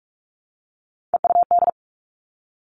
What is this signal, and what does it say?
Morse code, heard as a keyed tone, about 46 wpm, 700 Hz. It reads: EUD
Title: Three Challenging Letter-Number Combinations 45wpm